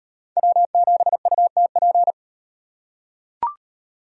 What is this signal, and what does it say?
Morse code, heard as a keyed tone, about 35 wpm, 700 Hz. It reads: W7UTP